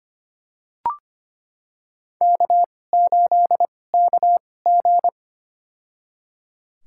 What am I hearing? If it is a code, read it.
Morse code, heard as a keyed tone, about 25 wpm, spaced as for 18 wpm, 700 Hz. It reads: K8KG
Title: USA Call Signs - 4 Characters - 2x Character Spacing - 25wpm